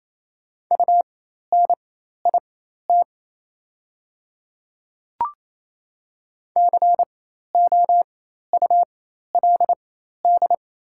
Morse code, heard as a keyed tone, about 28 wpm, spaced as for 13 wpm, 700 Hz. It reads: UNIT COULD
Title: 4x Character Spacing - Top 500 Words - 28wpm